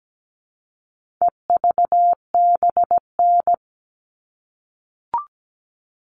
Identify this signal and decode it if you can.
Morse code, heard as a keyed tone, about 17 wpm, 700 Hz. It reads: EVBN
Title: All Cut Numbers - 4-Digit 17wpm